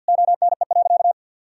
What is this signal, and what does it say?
Morse code, heard as a keyed tone, about 52 wpm, 700 Hz. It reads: ONE.
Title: Sentences from Top 1000 Words - Review Part 5 - 50wpm